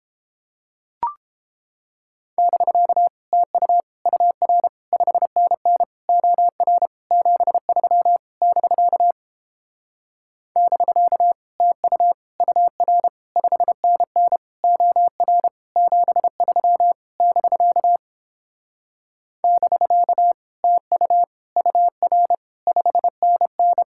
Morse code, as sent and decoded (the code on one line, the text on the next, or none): -...-.- / - ..- / ..- .-. / ..... -. -. / --- .-. / --... ...-- / -...-.- / -...-.- / - ..- / ..- .-. / ..... -. -. / --- .-. / --... ...-- / -...-.- / -...-.- / - ..- / ..- .-. / ..... -. -.
<BK> TU UR 5NN OR 73 <BK> <BK> TU UR 5NN OR 73 <BK> <BK> TU UR 5NN